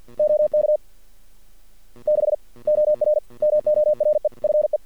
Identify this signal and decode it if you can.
Morse code, heard as a keyed tone, about 51 wpm, 600 Hz. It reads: OK 6 OR MORE FE
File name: archived / CRSnet